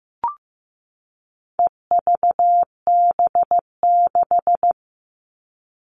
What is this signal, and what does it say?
Morse code, heard as a keyed tone, about 15 wpm, 700 Hz. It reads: EVB6
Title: All Cut Numbers - 4-Digit 15wpm